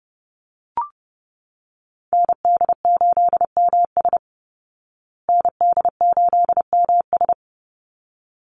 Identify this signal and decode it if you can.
Morse code, heard as a keyed tone, about 30 wpm, 700 Hz. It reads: ND8MS ND8MS
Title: Call Signs Repeated 3x - 30wpm